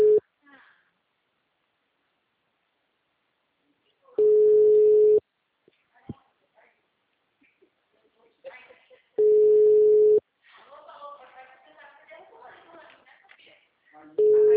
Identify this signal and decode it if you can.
Ringback tone (Europe, 425 Hz), with speech.
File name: out- -2008- - - .64288